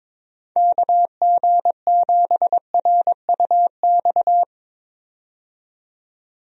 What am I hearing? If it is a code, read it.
Morse code, heard as a keyed tone, about 22 wpm, 700 Hz. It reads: KG7RUX
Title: USA Call Signs - 6 Characters - 22wpm